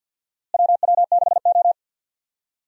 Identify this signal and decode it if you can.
Morse code, heard as a keyed tone, about 48 wpm, 700 Hz. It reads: WW6O
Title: USA Call Signs - 4 Characters - Repeated 3x - 50wpm